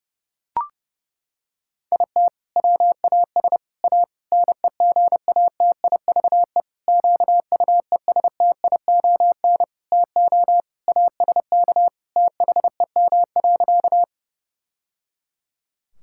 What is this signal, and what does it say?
Morse code, heard as a keyed tone, about 32 wpm, 700 Hz. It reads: IT WAS A NEGATIVE QUESTION TO ASK THEM.
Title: Sentences from Top 1000 Words - Review Part 6 - 30wpm